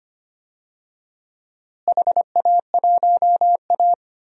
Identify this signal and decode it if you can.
Morse code, heard as a keyed tone, about 25 wpm, 700 Hz. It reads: HA1A
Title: DXCC-100 Call Signs - 4 Characters - Repeated 2x - 25wpm